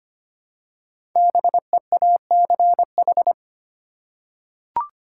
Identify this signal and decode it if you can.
Morse code, heard as a keyed tone, about 25 wpm, 700 Hz. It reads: BEACH
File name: 5 Letter Words - First 3 Letters Given - 25wpm